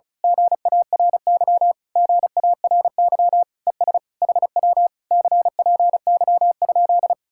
Morse code, heard as a keyed tone, about 33 wpm, 700 Hz. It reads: GARY GARY ES HW CPY?
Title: Speed-Racing Common QSO Phrases - 35wpm